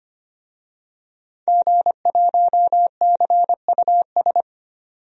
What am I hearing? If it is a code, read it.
Morse code, heard as a keyed tone, about 25 wpm, 700 Hz. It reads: G1CUS
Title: DXCC-100 Call Signs - 5 Characters - 25wpm